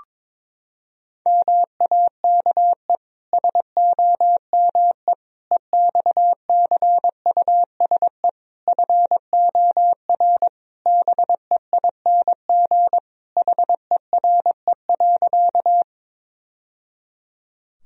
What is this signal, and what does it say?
Morse code, heard as a keyed tone, about 22 wpm, 700 Hz. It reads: MAKE SOME EXCUSE FOR BEING HERE.
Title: Sentences from Top 1000 Words - Review Part 3 - 22wpm